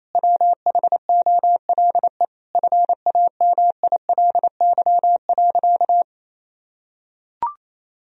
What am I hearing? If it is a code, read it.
Morse code, heard as a keyed tone, about 28 wpm, 700 Hz. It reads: WHOLE FAMILY.
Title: Sentences from Top 1000 Words - Review Part 1 - 28wpm